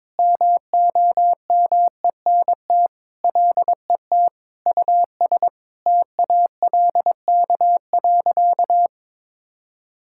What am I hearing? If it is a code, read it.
Morse code, heard as a keyed tone, about 22 wpm, 700 Hz. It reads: MOMENT LET US TALK.
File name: practice / 22wpm / Sentences with 4-Word Phrases - 22wpm